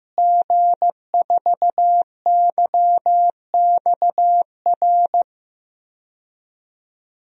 Morse code, heard as a keyed tone, about 15 wpm, 700 Hz. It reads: G4YXR
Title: DXCC-100 Call Signs - 5 Characters - Repeated 2x - 15wpm